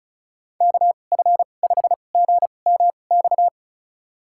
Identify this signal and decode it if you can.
Morse code, heard as a keyed tone, about 36 wpm, spaced as for 25 wpm, 700 Hz. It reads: KF5GMX
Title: USA Call Signs - 6 Characters - 2x Character Spacing - 35wpm